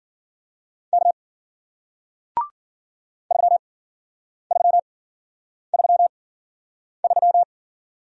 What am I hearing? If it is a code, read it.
Morse code, heard as a keyed tone, about 54 wpm, spaced as for 9 wpm, 700 Hz. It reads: K 3333